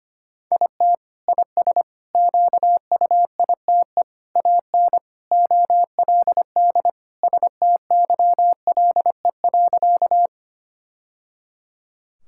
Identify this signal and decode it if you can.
Morse code, heard as a keyed tone, about 25 wpm, 700 Hz. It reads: IT IS QUITE AN OLD STYLE.
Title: Sentences from Top 1000 Words - Review Part 5 - 25wpm